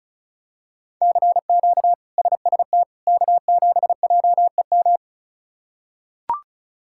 Morse code, heard as a keyed tone, about 36 wpm, 700 Hz. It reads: CQ SST K7JEM
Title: K1USN SST QSO Practice - 35wpm